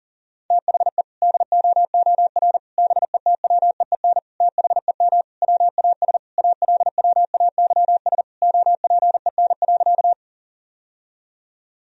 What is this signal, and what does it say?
Morse code, heard as a keyed tone, about 40 wpm, 700 Hz. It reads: THE DOOR BETWEEN THEM WAS ALWAYS OPEN.
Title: Sentences from Top 1000 Words - Review Part 1 - 40wpm